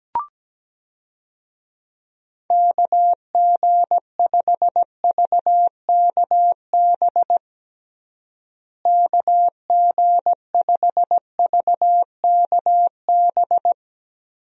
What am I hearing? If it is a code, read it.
Morse code, heard as a keyed tone, about 17 wpm, 700 Hz. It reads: KG5VKB KG5VKB
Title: USA Call Signs - 6 Characters - Repeated 3x - 17wpm